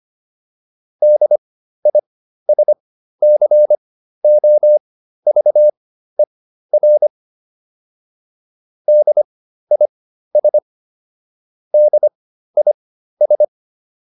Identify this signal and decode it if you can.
Morse code, heard as a keyed tone, about 25 wpm, spaced as for 13 wpm, 600 Hz. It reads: DISCOVER DIS DIS